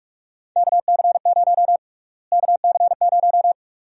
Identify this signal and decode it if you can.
Morse code, heard as a keyed tone, about 42 wpm, 700 Hz. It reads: KC0 KC0
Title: USA Call Sign Prefixes - Repeated 2x - 45wpm